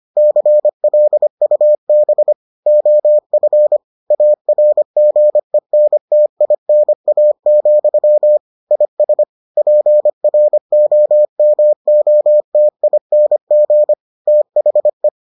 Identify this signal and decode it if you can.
Morse code, heard as a keyed tone, about 25 wpm, 600 Hz. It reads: CLUB OF ARGENTINA, IS PROMOTING THE